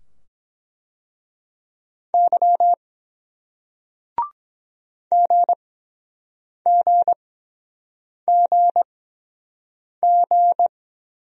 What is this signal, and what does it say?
Morse code, heard as a keyed tone, about 23 wpm, spaced as for 7 wpm, 700 Hz. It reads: Y GGGG